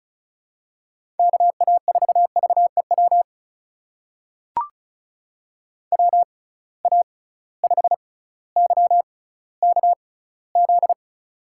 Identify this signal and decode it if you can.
Morse code, heard as a keyed tone, about 36 wpm, spaced as for 18 wpm, 700 Hz. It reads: KA4VEW WA5YKZ
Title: USA Call Signs - 6 Characters - 6x Character Spacing - 35wpm